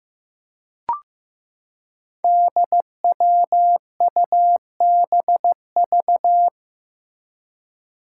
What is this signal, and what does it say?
Morse code, heard as a keyed tone, about 15 wpm, 700 Hz. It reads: DWUBV
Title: Five Challenging Letter-Number Combinations 15wpm